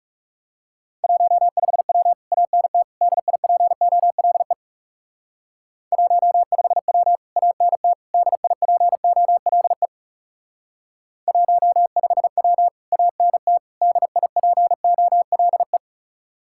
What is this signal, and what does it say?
Morse code, heard as a keyed tone, about 38 wpm, 700 Hz. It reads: 15W ANT DIPOLE 15W ANT DIPOLE 15W ANT DIPOLE
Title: Speed-Racing Common QSO Phrases - 30wpm